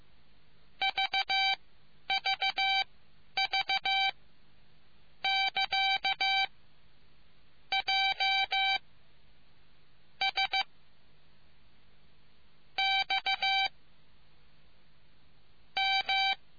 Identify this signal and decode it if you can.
Morse code, heard as a keyed tone, about 15 wpm, spaced as for 10 wpm, 780 Hz. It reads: VVV <KA> J S X M